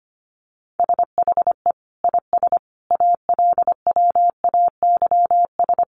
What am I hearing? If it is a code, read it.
Morse code, heard as a keyed tone, about 25 wpm, 700 Hz. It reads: SHE IS ALWAYS
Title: Sentences from Top 1400 Words 25wpm